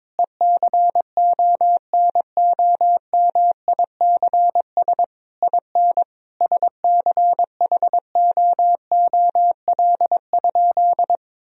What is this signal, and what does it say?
Morse code, heard as a keyed tone, about 21 wpm, 700 Hz. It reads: ECONOMICS IN SCHOOL?
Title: Sentences from Top 1000 Words - Review Part 3 - 22wpm